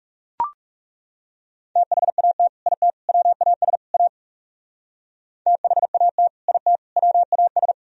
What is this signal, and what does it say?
Morse code, heard as a keyed tone, about 39 wpm, 700 Hz. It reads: THAT IT WAS A THAT IT WAS